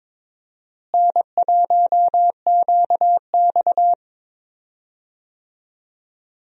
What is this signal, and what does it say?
Morse code, heard as a keyed tone, about 21 wpm, 700 Hz. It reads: N1QX
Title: USA Call Signs - 4 Characters - Repeated 3x - 22wpm